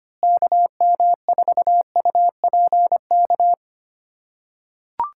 Morse code, heard as a keyed tone, about 25 wpm, 700 Hz. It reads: KM4UPK
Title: USA Callsign DE Callsign - 6 Characters - 25wpm